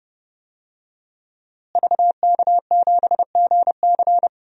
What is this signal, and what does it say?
Morse code, heard as a keyed tone, about 29 wpm, 700 Hz. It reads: VK7GC
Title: DXCC-100 Call Signs - 5 Characters - Repeated 3x - 30wpm